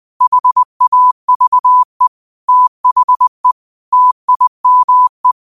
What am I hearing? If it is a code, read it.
Morse code, heard as a keyed tone, about 20 wpm, 1000 Hz. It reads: HAVE THE TIME